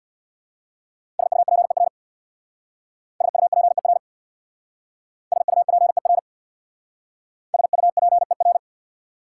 Morse code, heard as a keyed tone, about 66 wpm, 690 Hz. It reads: SUPER SUPER SUPER SUPER